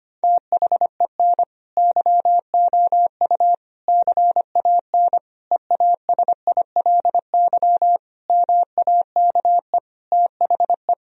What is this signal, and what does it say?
Morse code, heard as a keyed tone, about 25 wpm, 700 Hz. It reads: THEN YOU CAN EASILY MAKE THE
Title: Sentences from Top 1000 Words - Review Part 1 - 25wpm